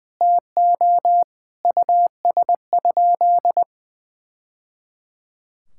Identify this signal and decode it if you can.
Morse code, heard as a keyed tone, about 21 wpm, 700 Hz. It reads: TO US?
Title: Sentences from Top 1000 Words - Review Part 6 - 20wpm